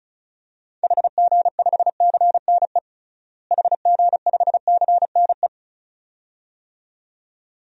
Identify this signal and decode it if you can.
Morse code, heard as a keyed tone, about 36 wpm, 700 Hz. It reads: HG5CNE HG5CNE
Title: DXCC-100 Call Signs - 6 Characters - Repeated 2x - 35wpm